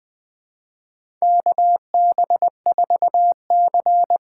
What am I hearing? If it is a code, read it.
Morse code, heard as a keyed tone, about 20 wpm, 700 Hz. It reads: KB4C
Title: USA Call Signs - 6 Characters - 6x Character Spacing - 20wpm